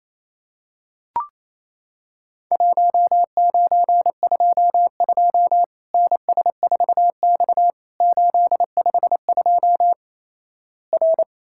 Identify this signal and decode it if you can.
Morse code, heard as a keyed tone, about 28 wpm, 700 Hz. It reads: 1922 NS4X 852 R
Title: Worked All Europe DX Contest - QTC Exchange - 28wpm